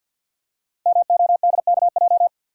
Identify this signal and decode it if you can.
Morse code, heard as a keyed tone, about 49 wpm, 700 Hz. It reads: MODKJ
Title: Code Groups - 5 Letter:Numbers - 50wpm